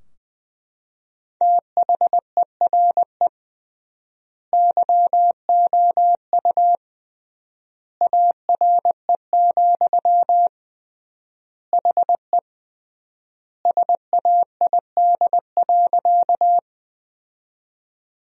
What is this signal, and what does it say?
Morse code, heard as a keyed tone, about 20 wpm, 700 Hz. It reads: THERE YOU ARE, HE SAID.
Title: 3x Word-Spacing - Sentences from Top 100 Words - 20wpm